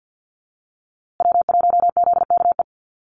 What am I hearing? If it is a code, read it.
Morse code, heard as a keyed tone, about 50 wpm, 700 Hz. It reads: W1ZKE
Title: USA Call Signs Repeated 3x 50wpm